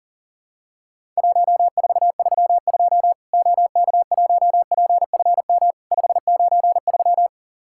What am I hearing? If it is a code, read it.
Morse code, heard as a keyed tone, about 42 wpm, 700 Hz. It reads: 1432 OK1PFM 593
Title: Worked All Europe DX Contest - QTC Exchange - 40wpm